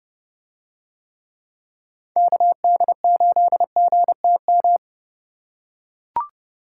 Morse code, heard as a keyed tone, about 29 wpm, 700 Hz. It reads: KD8GTM